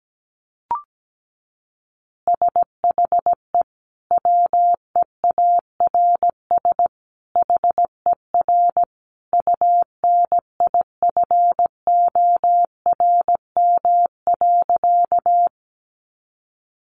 Morse code, heard as a keyed tone, about 17 wpm, 700 Hz. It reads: SHE WEARS HER UNIFORM.
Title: Easy Phrases 17wpm